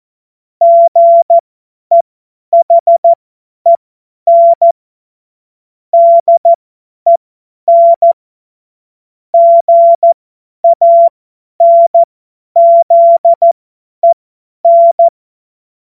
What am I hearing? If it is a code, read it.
Morse code, heard as a keyed tone, about 14 wpm, spaced as for 10 wpm, 680 Hz. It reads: GEHEN DEN GANZEN
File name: Morsekurs / Witze / tz 08. November 2022 50 F70